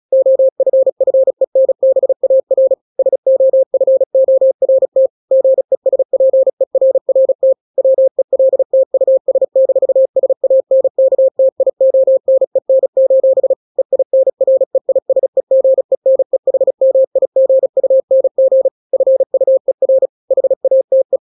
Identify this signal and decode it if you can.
Morse code, heard as a keyed tone, about 36 wpm, 530 Hz. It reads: OFFENBAR SOFORT GESPERRT WELT US-SANKTIONEN: EINREISEGENEHMIGUNG FUER HATE
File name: daily / 35wpm